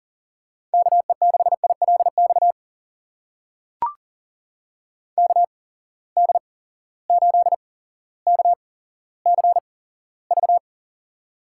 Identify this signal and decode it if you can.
Morse code, heard as a keyed tone, about 40 wpm, spaced as for 17 wpm, 700 Hz. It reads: KE6ILX KD8KCV
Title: USA Call Signs - 6 Characters - 8x Character Spacing - 40wpm